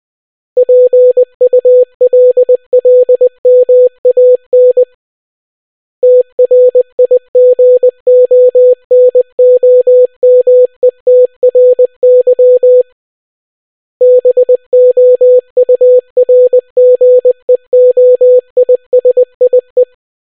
Morse code, heard as a keyed tone, about 20 wpm, 500 Hz. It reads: PULLMAN TRIGONOMETRY BOURGEOISIE